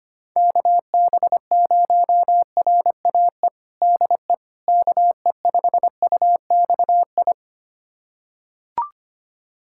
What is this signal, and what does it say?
Morse code, heard as a keyed tone, about 25 wpm, 700 Hz. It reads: KB0RAE DE KE5UXI
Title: USA Callsign DE Callsign 25wpm